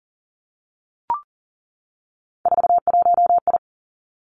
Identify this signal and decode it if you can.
Morse code, heard as a keyed tone, about 38 wpm, 690 Hz. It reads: 41I
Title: Three Challenging Letter-Number Combinations 40wpm